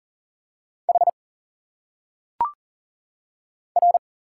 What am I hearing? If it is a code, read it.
Morse code, heard as a keyed tone, about 40 wpm, 700 Hz. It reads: H R
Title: Single Letters - Phonetic Pronunciation 40wpm